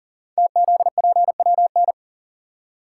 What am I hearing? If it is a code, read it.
Morse code, heard as a keyed tone, about 41 wpm, 700 Hz. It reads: TZPWN
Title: Code Groups - 5 Letter:Numbers - 40wpm